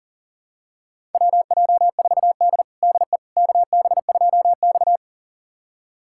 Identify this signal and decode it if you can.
Morse code, heard as a keyed tone, about 40 wpm, 700 Hz. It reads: WJ4D DE KB2X
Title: USA Callsign DE Callsign - 4 Characters - 40wpm